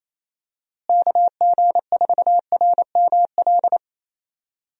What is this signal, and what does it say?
Morse code, heard as a keyed tone, about 28 wpm, 700 Hz. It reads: KG4RML
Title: USA Call Signs - 6 Characters - Repeated 2x - 28wpm